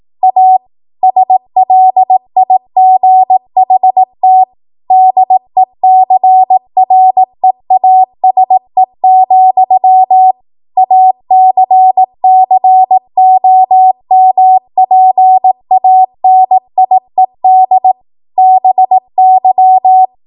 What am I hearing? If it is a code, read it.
Morse code, heard as a keyed tone, about 18 wpm, 750 Hz. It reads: A SLIGHT DECREASE, ACCOMPANIED BY
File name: Morse / 18WPM